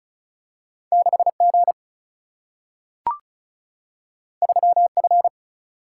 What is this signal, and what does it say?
Morse code, heard as a keyed tone, about 36 wpm, 700 Hz. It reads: 6G 3F
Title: Single-Letter Single-Number 35wpm